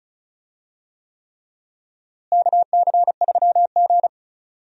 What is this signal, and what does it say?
Morse code, heard as a keyed tone, about 34 wpm, 700 Hz. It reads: KC3G
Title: USA Call Signs - 4 Characters - Repeated 3x - 35wpm